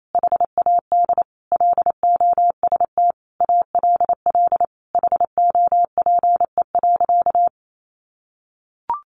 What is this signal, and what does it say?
Morse code, heard as a keyed tone, about 28 wpm, 700 Hz. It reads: HAD LOST ALL HOPE.
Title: Sentences from Top 1400 Words 28wpm